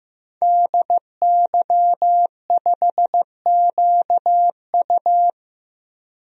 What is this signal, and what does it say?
Morse code, heard as a keyed tone, about 15 wpm, 700 Hz. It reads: DY5QU
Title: DXCC-100 Call Signs - 5 Characters - 15wpm